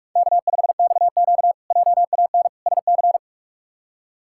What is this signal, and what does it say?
Morse code, heard as a keyed tone, about 41 wpm, 700 Hz. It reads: K5XQ JAN SC
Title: K1USN SST QSO Practice - 45wpm